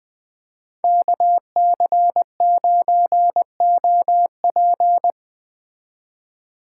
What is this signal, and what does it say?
Morse code, heard as a keyed tone, about 20 wpm, 700 Hz. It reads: KC9OP
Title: Speed Racing - USA 2x2 Call Signs 17wpm